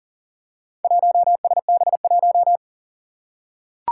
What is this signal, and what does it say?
Morse code, heard as a keyed tone, about 40 wpm, 700 Hz. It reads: 1SB1
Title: Four Challenging Letter-Number Combinations 40wpm